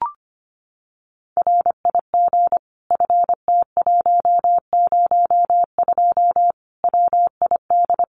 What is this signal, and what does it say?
Morse code, heard as a keyed tone, about 25 wpm, 700 Hz. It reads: RIG FT102 WID
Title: Common QSO Phrases 25wpm